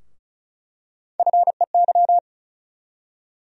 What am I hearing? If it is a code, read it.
Morse code, heard as a keyed tone, about 36 wpm, 700 Hz. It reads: FEY
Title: USA Call Sign Postfixes - 3 Letters 35wpm